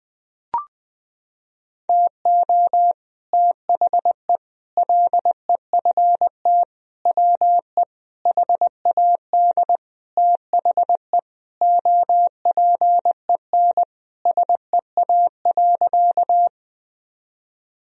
Morse code, heard as a keyed tone, about 20 wpm, 700 Hz. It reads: TO THE LEFT WE HAD THE OPEN SEA.
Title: Sentences from Top 1000 Words - Review Part 1 - 20wpm